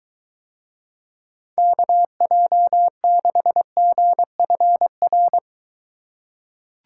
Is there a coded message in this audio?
Yes: Morse code from 1.6 to 5.4 s, heard as a keyed tone, about 23 wpm, 700 Hz. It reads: KJ6GFR